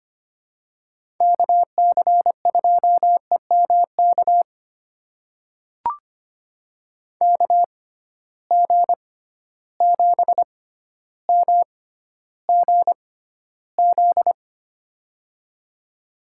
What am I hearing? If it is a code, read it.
Morse code, heard as a keyed tone, about 25 wpm, spaced as for 13 wpm, 700 Hz. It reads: KC2EMK KG7MGZ
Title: USA Call Signs - 6 Characters - 6x Character Spacing - 25wpm